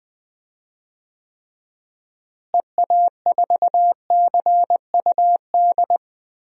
Morse code, heard as a keyed tone, about 20 wpm, 700 Hz. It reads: EA4CUD
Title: DXCC-100 Call Signs - 6 Characters - Repeated 2x - 20wpm